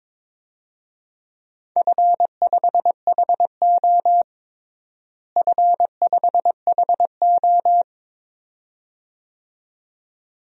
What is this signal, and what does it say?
Morse code, heard as a keyed tone, about 22 wpm, 700 Hz. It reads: F5HO F5HO
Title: DXCC-100 Call Signs - 4 Characters - Repeated 2x - 22wpm